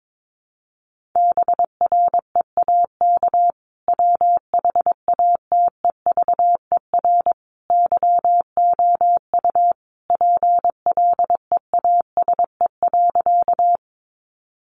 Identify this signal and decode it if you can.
Morse code, heard as a keyed tone, about 22 wpm, 700 Hz. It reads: BREAK WHATEVER YOU PLEASE.
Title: Sentences from Top 1500 Words 22wpm